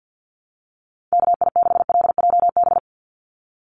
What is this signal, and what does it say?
Morse code, heard as a keyed tone, about 50 wpm, 700 Hz. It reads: KI6LJB
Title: USA Call Signs 50wpm